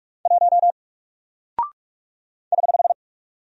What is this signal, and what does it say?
Morse code, heard as a keyed tone, about 47 wpm, 700 Hz. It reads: 1 <HH>
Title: Single Character (Letter:Number:Punctuation:Prosign) Alt 45wpm